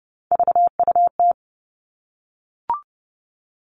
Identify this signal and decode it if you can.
Morse code, heard as a keyed tone, about 30 wpm, 700 Hz. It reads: VUT
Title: Three Challenging Letter-Number Combinations 30wpm